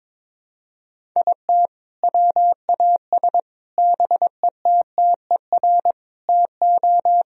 Morse code, heard as a keyed tone, about 22 wpm, 700 Hz. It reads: IT WAS BETTER TO
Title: Sentences from Top 1000 Words - Review Part 1 - 22wpm